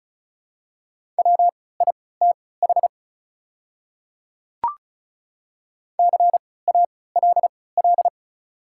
Morse code, heard as a keyed tone, about 36 wpm, spaced as for 20 wpm, 690 Hz. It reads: WITH CALL
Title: 3x Character Spacing - Top 100 Words - 35wpm